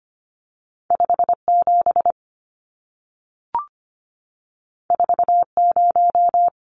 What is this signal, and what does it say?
Morse code, heard as a keyed tone, about 25 wpm, 700 Hz. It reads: 57 40
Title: Numbers - 2-Digit 25wpm